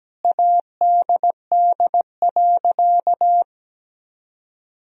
Morse code, heard as a keyed tone, about 17 wpm, 700 Hz. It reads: ADD.
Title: Sentences from Top 1000 Words - Review Part 1 - 17wpm